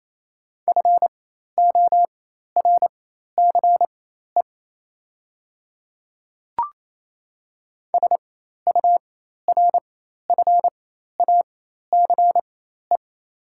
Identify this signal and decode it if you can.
Morse code, heard as a keyed tone, about 27 wpm, spaced as for 13 wpm, 700 Hz. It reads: FORCE SURFACE
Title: 4x Character Spacing - Top 500 Words - 28wpm